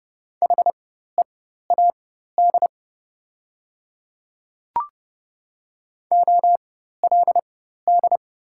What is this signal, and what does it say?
Morse code, heard as a keyed tone, about 29 wpm, spaced as for 14 wpm, 700 Hz. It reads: HEAD OLD
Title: 4x Character Spacing - Top 300 Words - 30wpm